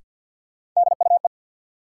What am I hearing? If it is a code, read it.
Morse code, heard as a keyed tone, about 53 wpm, 700 Hz. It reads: DUE
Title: Sets of 5 Words Encore - New Word Drill - 50wpm